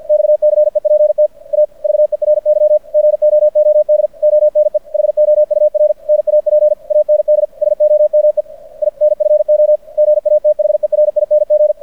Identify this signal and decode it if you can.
Morse code, heard as a keyed tone, about 53 wpm, 600 Hz. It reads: XCEPT A VERY GOOD ONE HOUR NAP AND SOME INFO GATHERING
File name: archived / CRSnet